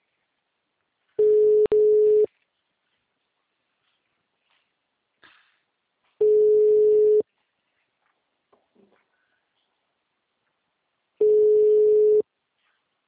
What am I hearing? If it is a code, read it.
Ringback tone (Europe, 425 Hz).
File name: out- -2002- - - .95926